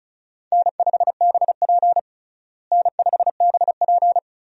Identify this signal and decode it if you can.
Morse code, heard as a keyed tone, about 36 wpm, 700 Hz. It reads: N5BP N5BP
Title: USA Call Signs - 4 Characters - Repeated 3x - 35wpm